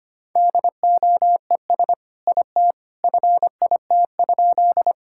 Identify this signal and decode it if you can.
Morse code, heard as a keyed tone, about 25 wpm, 700 Hz. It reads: DOES IT FIT?
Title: Sentences from Top 1000 Words - Review Part 5 - 25wpm